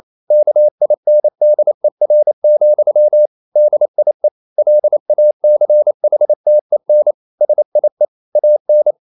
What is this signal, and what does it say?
Morse code, heard as a keyed tone, about 28 wpm, 600 Hz. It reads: KINDER, DIE LACHTEN SIE AN
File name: marienkind9 28wpm